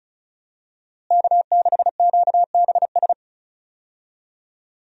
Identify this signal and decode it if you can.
Morse code, heard as a keyed tone, about 36 wpm, 700 Hz. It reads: K6QBS
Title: USA Call Signs - 5 Characters - Repeated 2x - 35wpm